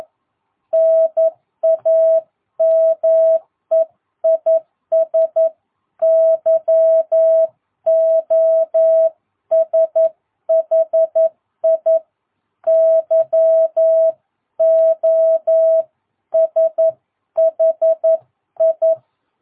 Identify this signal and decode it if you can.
Morse code, heard as a keyed tone, about 11 wpm, 660 Hz. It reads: NAME IS YOSHI YOSHI
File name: jh8cbh / 2022cw